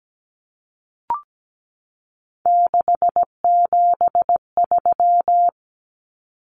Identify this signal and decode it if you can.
Morse code, heard as a keyed tone, about 17 wpm, 700 Hz. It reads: 673
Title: Common Cut Numbers - 3-Digit 17wpm